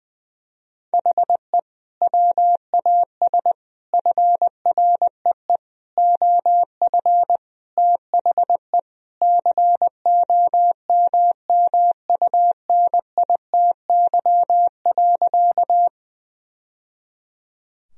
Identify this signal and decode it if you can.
Morse code, heard as a keyed tone, about 20 wpm, 700 Hz. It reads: HE WAS FREE OF THE COMMUNITY.
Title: Sentences from Top 1000 Words - Review Part 4 - 20wpm